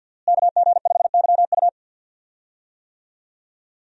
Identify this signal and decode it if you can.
Morse code, heard as a keyed tone, about 52 wpm, 700 Hz. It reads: KG5YU
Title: USA Call Signs - 5 Characters - Repeated 2x - 50wpm